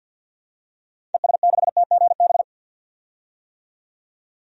Morse code, heard as a keyed tone, about 46 wpm, 700 Hz. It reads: ES6TGB
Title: DXCC-100 Call Signs - 6 Characters - 50wpm